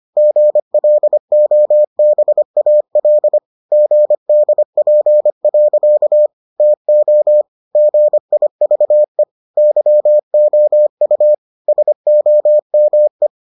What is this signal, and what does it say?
Morse code, heard as a keyed tone, about 25 wpm, 600 Hz. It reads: GLOBAL GDP. TO GIVE YOU SOME